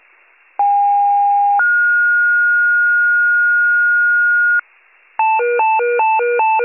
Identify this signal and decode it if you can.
Two-tone page, heard as a one-second tone, then a three-second tone, about 810 then 1430 Hz, with noise. It is heard from 0.6 to 4.6 s.